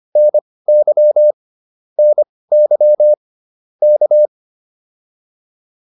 Morse code, heard as a keyed tone, about 25 wpm, spaced as for 18 wpm, 600 Hz. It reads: NY NY K
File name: POTA208 18